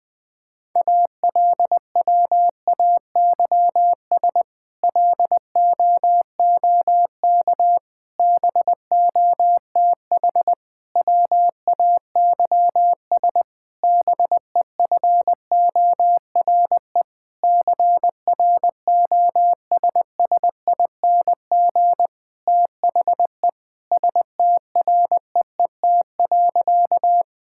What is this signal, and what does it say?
Morse code, heard as a keyed tone, about 20 wpm, 700 Hz. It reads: ALWAYS LOOK BOTH WAYS BEFORE CROSSING THE STREET.
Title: Difficult Sentences with Hints - 20wpm